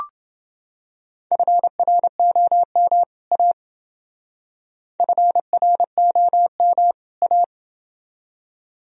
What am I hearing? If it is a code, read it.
Morse code, heard as a keyed tone, about 29 wpm, 700 Hz. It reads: FROM A FROM A